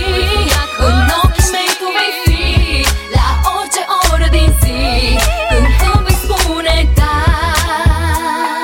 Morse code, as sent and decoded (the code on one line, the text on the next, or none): none